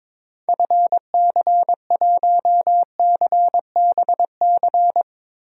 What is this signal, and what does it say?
Morse code, heard as a keyed tone, about 22 wpm, 700 Hz. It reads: FC1CBC
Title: DXCC-100 Call Signs - 6 Characters - 22wpm